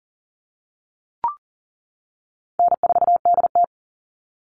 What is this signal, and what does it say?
Morse code, heard as a keyed tone, about 41 wpm, 700 Hz. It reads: N4DT
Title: Easy Call Signs - 40wpm